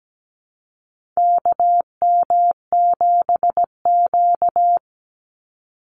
Morse code, heard as a keyed tone, about 17 wpm, 700 Hz. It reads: KM7Q
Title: Call Signs - 17wpm